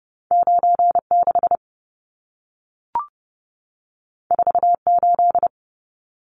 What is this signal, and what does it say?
Morse code, heard as a keyed tone, about 30 wpm, 700 Hz. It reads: 96 48
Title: Numbers - 2-Digit 30wpm